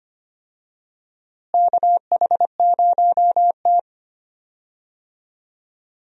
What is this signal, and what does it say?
Morse code, heard as a keyed tone, about 25 wpm, 700 Hz. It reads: KH0T
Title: USA Call Signs - 4 Characters - Repeated 3x - 25wpm